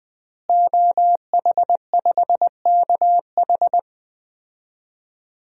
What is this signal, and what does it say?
Morse code, heard as a keyed tone, about 21 wpm, 700 Hz. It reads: OH5KH
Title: DXCC-100 Call Signs - 5 Characters - Repeated 2x - 20wpm